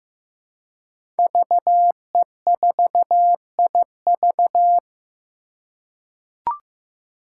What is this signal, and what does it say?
Morse code, heard as a keyed tone, about 16 wpm, 700 Hz. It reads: VE4IV
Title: Five Challenging Letter-Number Combinations 15wpm